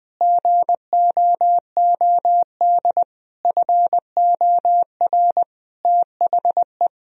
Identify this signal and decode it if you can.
Morse code, heard as a keyed tone, about 20 wpm, 700 Hz. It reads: GOOD FOR THE
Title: Sentences from Top 1000 Words - Review Part 4 - 20wpm